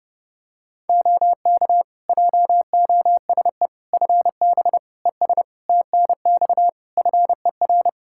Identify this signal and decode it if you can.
Morse code, heard as a keyed tone, about 30 wpm, 700 Hz. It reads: OK JOSE FB ES TNX FER
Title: K1USN SST QSO Practice 30wpm